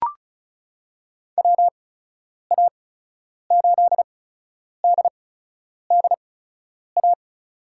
Morse code, heard as a keyed tone, about 36 wpm, spaced as for 10 wpm, 700 Hz. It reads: WA8DDA
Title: USA Call Signs - 8x Character Spacing - 35wpm